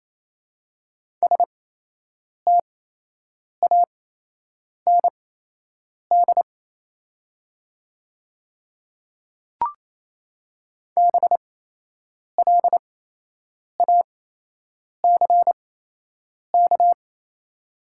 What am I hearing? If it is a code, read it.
Morse code, heard as a keyed tone, about 28 wpm, spaced as for 8 wpm, 700 Hz. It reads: STAND BLACK
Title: 8x Character Spacing - Top 500 Words - 28wpm